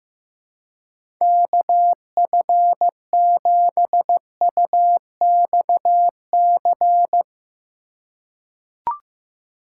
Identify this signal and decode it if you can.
Morse code, heard as a keyed tone, about 15 wpm, 700 Hz. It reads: KF7UXC
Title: K1USN SST QSO Practice - 15wpm